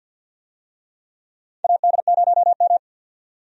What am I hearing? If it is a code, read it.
Morse code, heard as a keyed tone, about 48 wpm, 700 Hz. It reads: AD0M
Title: USA Call Signs - 4 Characters - 50wpm